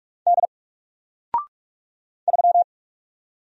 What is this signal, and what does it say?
Morse code, heard as a keyed tone, about 45 wpm, 700 Hz. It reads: D 3
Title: Single Letter-Number 45wpm